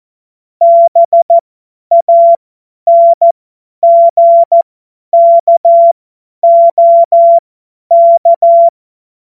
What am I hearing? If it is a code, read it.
Morse code, heard as a keyed tone, about 14 wpm, spaced as for 10 wpm, 680 Hz. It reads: BANGKOK
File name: Morsekurs / Witze / Witz 28. März 2023 50 F70